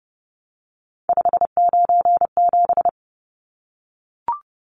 Morse code, heard as a keyed tone, about 30 wpm, 700 Hz. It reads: 597
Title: Numbers - 3-Digit 30wpm